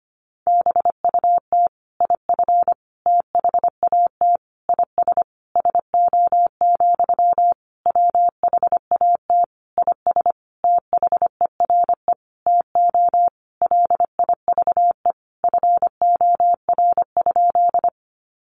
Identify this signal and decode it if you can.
Morse code, heard as a keyed tone, about 25 wpm, 700 Hz. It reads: BUT IF THAT IS SO, WHAT IS THERE TO LIVE FOR?
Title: Sentences from Top 200 Words 25wpm